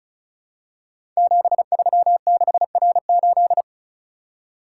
Morse code, heard as a keyed tone, about 36 wpm, 700 Hz. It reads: 736R8
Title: Code Groups - 5 Letter:Numbers - 35wpm